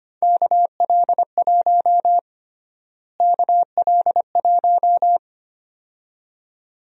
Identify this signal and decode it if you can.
Morse code, heard as a keyed tone, about 25 wpm, 700 Hz. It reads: KL1 KL1
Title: USA Call Sign Prefixes - Repeated 2x - 25wpm